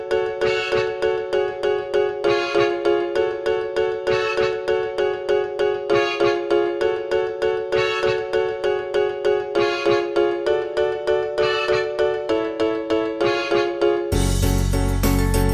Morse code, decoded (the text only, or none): none